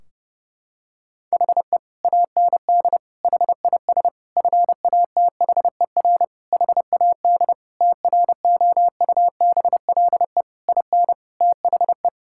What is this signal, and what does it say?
Morse code, heard as a keyed tone, about 30 wpm, 700 Hz. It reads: HE AND HIS FATHER HAD TROUBLE IN THE
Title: Sentences from Top 1600 Words 30wpm